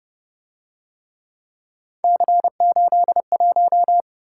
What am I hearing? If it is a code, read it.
Morse code, heard as a keyed tone, about 28 wpm, 700 Hz. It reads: C81